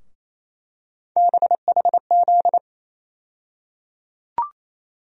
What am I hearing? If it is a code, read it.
Morse code, heard as a keyed tone, about 28 wpm, 700 Hz. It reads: BHZ
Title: USA Call Sign Postfixes - 3 Letters 28wpm